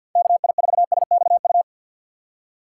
Morse code, heard as a keyed tone, about 47 wpm, 700 Hz. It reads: KI4SXU
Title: USA Call Signs - 6 Characters - 3x Character Spacing - 50wpm